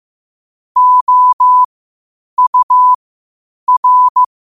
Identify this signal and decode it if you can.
Morse code, heard as a keyed tone, about 15 wpm, spaced as for 8 wpm, 1000 Hz. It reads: OUR
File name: qotdfiles / 8wpm